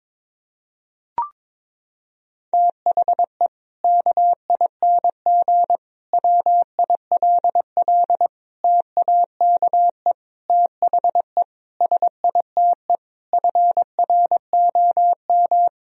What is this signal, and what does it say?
Morse code, heard as a keyed tone, about 22 wpm, 700 Hz. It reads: THE KING WILL TAKE THE SITE FROM
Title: Sentences from Top 1000 Words - Review Part 4 - 22wpm